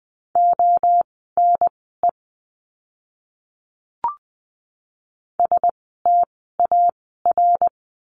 Morse code, heard as a keyed tone, about 20 wpm, spaced as for 15 wpm, 700 Hz. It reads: ONE STAR
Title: 054 Lesson 10f - Letter r - Words 20wpm